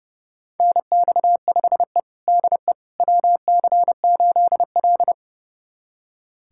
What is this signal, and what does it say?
Morse code, heard as a keyed tone, about 30 wpm, 700 Hz. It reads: NX5E DE WC8L
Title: USA Callsign DE Callsign - 4 Characters - 30wpm